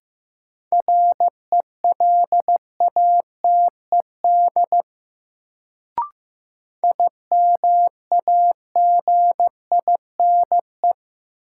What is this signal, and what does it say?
Morse code, heard as a keyed tone, about 15 wpm, 700 Hz. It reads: RELATED IMAGINE
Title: Sentences from Top 1400 Words - New Word Drill Alt - 15wpm